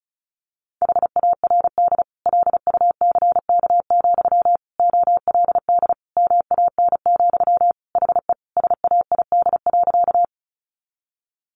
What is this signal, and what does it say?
Morse code, heard as a keyed tone, about 34 wpm, 700 Hz. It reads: HARD LUCK, OLD MAN, HE SAID.
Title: Sentences from Top 1500 Words 35wpm